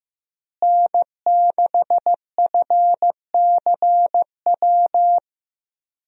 Morse code, heard as a keyed tone, about 15 wpm, 700 Hz. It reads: N6FCW
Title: USA Call Signs - 5 Characters - Repeated 3x - 15wpm